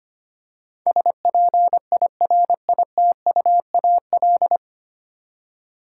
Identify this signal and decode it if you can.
Morse code, heard as a keyed tone, about 25 wpm, 700 Hz. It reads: SPIRITUAL
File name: practice / 25wpm / 9 Letter Words - First 3 Letters Given - 25wpm